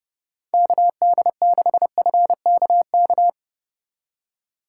Morse code, heard as a keyed tone, about 39 wpm, 700 Hz. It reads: KD6FKK